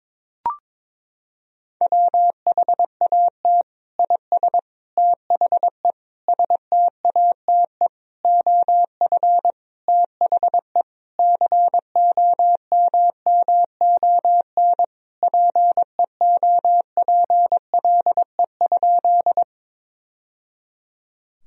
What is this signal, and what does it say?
Morse code, heard as a keyed tone, about 21 wpm, 700 Hz. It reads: WHAT IS THE STATE OF THE COMMON PEOPLE?
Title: Sentences from Top 1000 Words - Review Part 3 - 22wpm